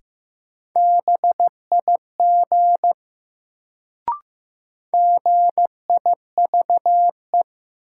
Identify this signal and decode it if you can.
Morse code, heard as a keyed tone, about 15 wpm, 700 Hz. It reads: BIG GIVE
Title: Sentences from Top 200 Words - New Word Drill - 15wpm